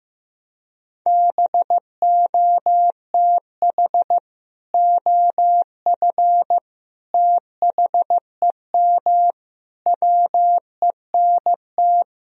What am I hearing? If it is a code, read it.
Morse code, heard as a keyed tone, about 15 wpm, 700 Hz. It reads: BOTH OF THEM WENT
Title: Sentences from Top 1000 Words - Review Part 1 - 15wpm